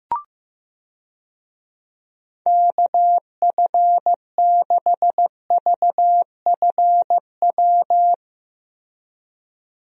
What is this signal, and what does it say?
Morse code, heard as a keyed tone, about 15 wpm, 700 Hz. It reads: KF6VFW
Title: USA Call Signs - 6 Characters - Repeated 3x - 15wpm